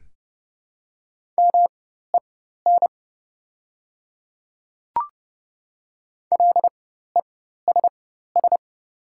Morse code, heard as a keyed tone, about 30 wpm, spaced as for 14 wpm, 700 Hz. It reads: MEN LESS
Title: 4x Character Spacing - Top 500 Words - 30wpm